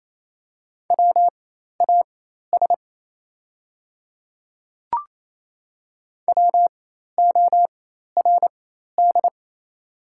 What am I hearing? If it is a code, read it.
Morse code, heard as a keyed tone, about 28 wpm, spaced as for 13 wpm, 700 Hz. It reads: WAS WORD
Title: 4x Character Spacing - Top 100 Words - 28wpm